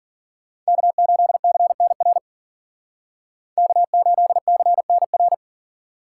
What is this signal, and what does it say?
Morse code, heard as a keyed tone, about 52 wpm, 700 Hz. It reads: K8CNR K8CNR